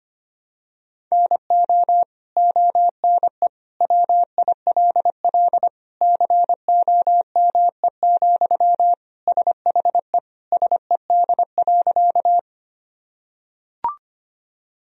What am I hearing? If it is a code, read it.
Morse code, heard as a keyed tone, about 25 wpm, 700 Hz. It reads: NO ONE WILL COME, SHE SED.
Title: Sentences from Top 100 Words and Abbreviations 25wpm